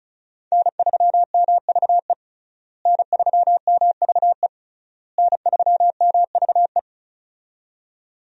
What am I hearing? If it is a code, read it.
Morse code, heard as a keyed tone, about 36 wpm, 700 Hz. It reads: N3MVE N3MVE N3MVE
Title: USA Call Signs - 5 Characters - Repeated 3x - 35wpm